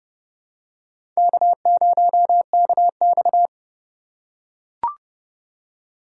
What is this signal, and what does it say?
Morse code, heard as a keyed tone, about 30 wpm, 700 Hz. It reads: K0KX
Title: K1USN SST QSO Practice - 30wpm